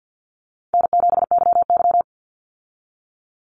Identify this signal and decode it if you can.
Morse code, heard as a keyed tone, about 48 wpm, 700 Hz. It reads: N7YY
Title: Easy Call Signs Repeated 3x - 50wpm